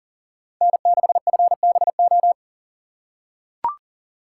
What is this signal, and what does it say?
Morse code, heard as a keyed tone, about 40 wpm, 700 Hz. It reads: N6FBO
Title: USA Call Signs - 6x Character Spacing - 40wpm